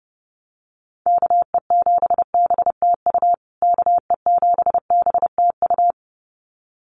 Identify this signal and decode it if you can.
Morse code, heard as a keyed tone, about 30 wpm, 700 Hz. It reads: KE7BTU KE7BTU
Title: USA Call Signs Repeated 2x 30wpm